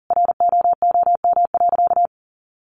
Morse code, heard as a keyed tone, about 37 wpm, 700 Hz. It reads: ROOM.
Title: Sentences from Top 1500 Words 40wpm